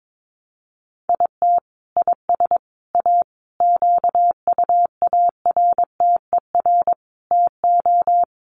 Morse code, heard as a keyed tone, about 22 wpm, 700 Hz. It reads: IT IS A QUARTER TO
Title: Sentences from Top 1300 Words 22wpm